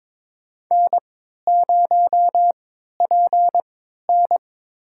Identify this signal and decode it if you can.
Morse code, heard as a keyed tone, about 22 wpm, spaced as for 13 wpm, 700 Hz. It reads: N0PN
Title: USA Call Signs - 3x Character Spacing - 22wpm